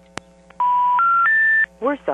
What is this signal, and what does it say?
Special information tone, with speech.
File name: TurtleCreekEWSD